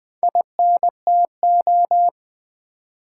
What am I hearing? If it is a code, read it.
Morse code, heard as a keyed tone, about 20 wpm, 700 Hz. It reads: INTO
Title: Build A Phrase - 2-Word Phrases - 20wpm